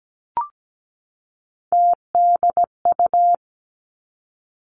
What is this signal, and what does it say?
Morse code, heard as a keyed tone, about 17 wpm, 700 Hz. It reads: TDU
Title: Three Challenging Letter-Number Combinations 17wpm